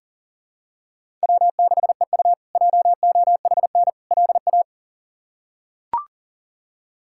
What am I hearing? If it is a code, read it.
Morse code, heard as a keyed tone, about 37 wpm, 700 Hz. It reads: W6EU JOHN LA
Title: K1USN SST QSO Practice - 40wpm